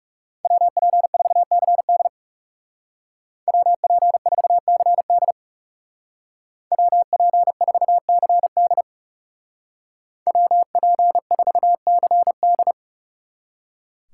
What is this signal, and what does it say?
Morse code, heard as a keyed tone, about 37 wpm, 700 Hz. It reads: WP4CD WP4CD WP4CD WP4CD
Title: Speed Racing - USA 2x2 Call Signs 30wpm